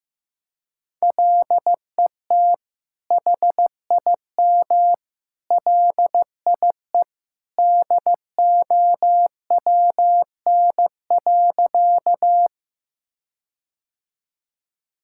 Morse code, heard as a keyed tone, about 16 wpm, 700 Hz. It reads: LET HIM LIE DOWN.
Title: Sentences from Top 1600 Words 15wpm